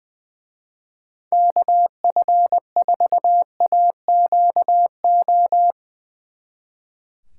Morse code, heard as a keyed tone, about 22 wpm, 700 Hz. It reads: KF4AQO
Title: Speed Racing - USA 2x3 Call Signs 20wpm